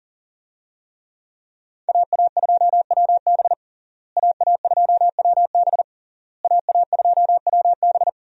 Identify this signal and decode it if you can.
Morse code, heard as a keyed tone, about 40 wpm, 700 Hz. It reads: AA2WB AA2WB AA2WB
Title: USA Call Signs - 5 Characters - Repeated 3x - 40wpm